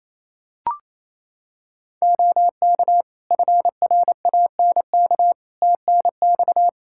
Morse code, heard as a keyed tone, about 28 wpm, 700 Hz. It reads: OK FRANK TNX
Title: Practice QSO - Exchange 4 - Ending - 28wpm